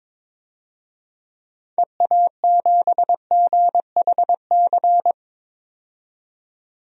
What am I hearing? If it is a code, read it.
Morse code, heard as a keyed tone, about 22 wpm, 700 Hz. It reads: EA7GHC
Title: DXCC-100 Call Signs - 6 Characters - 22wpm